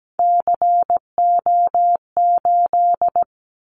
Morse code, heard as a keyed tone, about 17 wpm, 700 Hz. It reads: CO8
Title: Callsign Prefixes for Top 100 DXCC - 17wpm